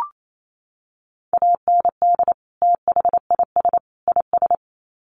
Morse code, heard as a keyed tone, about 28 wpm, 700 Hz. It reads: AND THIS IS
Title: Sets of 3 Words 28wpm